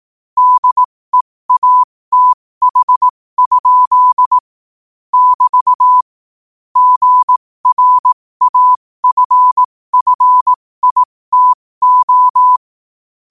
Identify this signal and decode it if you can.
Morse code, heard as a keyed tone, about 18 wpm, spaced as for 15 wpm, 1000 Hz. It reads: DEATH? <BT> GRAFFITO
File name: qotdfiles / 15wpm